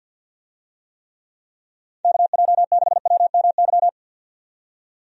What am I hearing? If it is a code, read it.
Morse code, heard as a keyed tone, about 48 wpm, 700 Hz. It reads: KJ6WMY
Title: USA Call Signs - 6 Characters - 50wpm